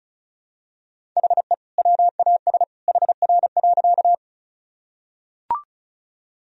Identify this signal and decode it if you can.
Morse code, heard as a keyed tone, about 35 wpm, 700 Hz. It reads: HE WAS HR.
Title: Sentences from Top 200 Words and Abbreviations 35wpm